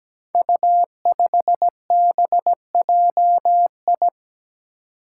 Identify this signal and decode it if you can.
Morse code, heard as a keyed tone, about 17 wpm, 700 Hz. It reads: U5BJI
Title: Five Challenging Letter-Number Combinations 17wpm